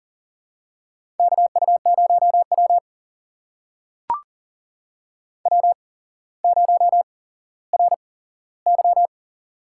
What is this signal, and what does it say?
Morse code, heard as a keyed tone, about 38 wpm, spaced as for 17 wpm, 700 Hz. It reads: KU0W W0RY
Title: USA Call Signs - 4 Characters - 8x Character Spacing - 40wpm